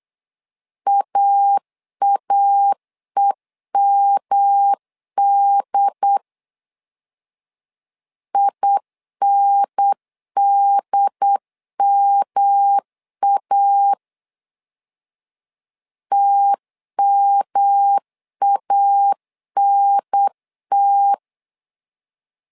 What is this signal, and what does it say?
Morse code, heard as a keyed tone, about 8 wpm, 800 Hz. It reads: AAEMD INDMA TMANT
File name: LEZIONI / 3 MWRO / 3 M 8wpm 2300